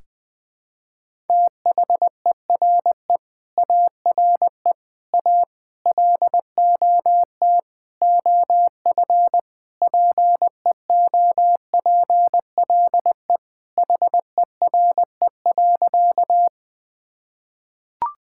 Morse code, heard as a keyed tone, about 20 wpm, 700 Hz. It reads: THERE ARE A LOT OF PEOPLE HERE.
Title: Sentences with 6-Word Phrases - 20wpm